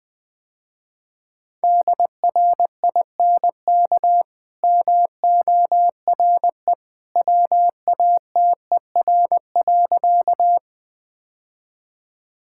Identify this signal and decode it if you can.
Morse code, heard as a keyed tone, about 20 wpm, 700 Hz. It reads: DRINK MORE WATER.
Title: Easy Sentences with Hints - 20wpm